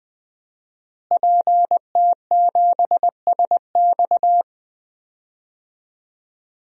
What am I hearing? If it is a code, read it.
Morse code, heard as a keyed tone, about 20 wpm, 700 Hz. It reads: PT7SX
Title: DXCC-100 Call Signs - 5 Characters - Repeated 2x - 20wpm